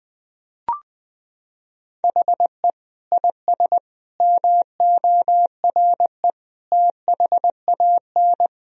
Morse code, heard as a keyed tone, about 20 wpm, 700 Hz. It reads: HE IS MORE THAN
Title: Sentences from Top 1000 Words - Review Part 3 - 20wpm